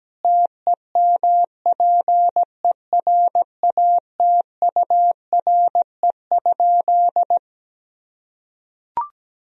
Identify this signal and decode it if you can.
Morse code, heard as a keyed tone, about 17 wpm, 700 Hz. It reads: TEMPERATURE?
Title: Sentences from Top 1000 Words - Review Part 4 - 17wpm